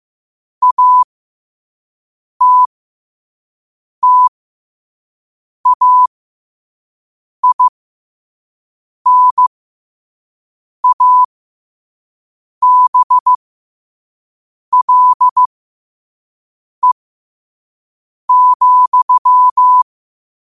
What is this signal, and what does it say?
Morse code, heard as a keyed tone, about 15 wpm, spaced as for 5 wpm, 1000 Hz. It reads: ATTAINABLE,
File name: qotdfiles / 5wpm